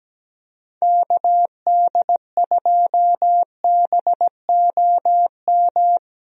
Morse code, heard as a keyed tone, about 17 wpm, 700 Hz. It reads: KD2BOM
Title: USA Call Signs - 6 Characters - Repeated 3x - 17wpm